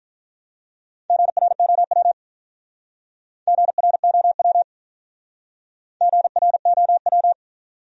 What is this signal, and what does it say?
Morse code, heard as a keyed tone, about 46 wpm, 700 Hz. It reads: GROW GROW GROW